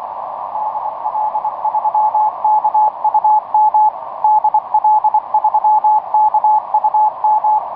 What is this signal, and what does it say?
Morse code, heard as a keyed tone, about 24 wpm, 860 Hz, with radio noise. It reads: DL3KUM DL3KUM
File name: dp1pol dl3kum etc 15m cw 17oct2009 1546